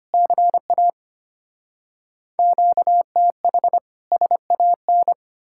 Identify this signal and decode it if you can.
Morse code, heard as a keyed tone, about 33 wpm, 700 Hz. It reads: CA QTH SAN